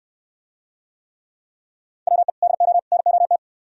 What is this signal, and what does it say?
Morse code, heard as a keyed tone, about 67 wpm, 700 Hz. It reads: WE DO NOT